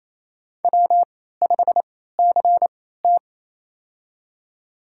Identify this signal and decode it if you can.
Morse code, heard as a keyed tone, about 28 wpm, spaced as for 16 wpm, 700 Hz. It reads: W5CT
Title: USA Call Signs - 4 Characters - 3x Character Spacing - 28wpm